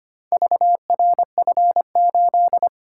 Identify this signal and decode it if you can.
Morse code, heard as a keyed tone, about 25 wpm, 700 Hz. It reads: VRF8
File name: practice / 25wpm / Four Letter-Number Combinations - 25wpm